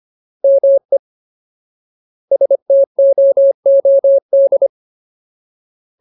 Morse code, heard as a keyed tone, about 25 wpm, 550 Hz. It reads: ME STOOD
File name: Ultimate Word List / 25WPM Ultimate